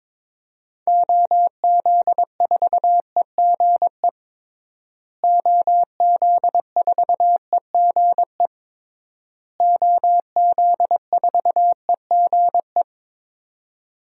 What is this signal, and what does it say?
Morse code, heard as a keyed tone, about 23 wpm, 700 Hz. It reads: OZ4EGE OZ4EGE OZ4EGE
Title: DXCC-100 Call Signs - 6 Characters - Repeated 3x - 22wpm